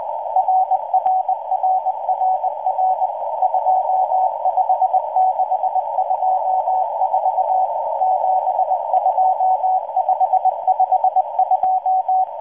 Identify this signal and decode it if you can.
Morse code, heard as a keyed tone, about 20 wpm, 750 Hz, with noise. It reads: RRRRR 25624 72552